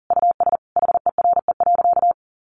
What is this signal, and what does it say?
Morse code, heard as a keyed tone, about 40 wpm, 700 Hz. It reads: US HERE.
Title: A Sound Of Thunder 40wpm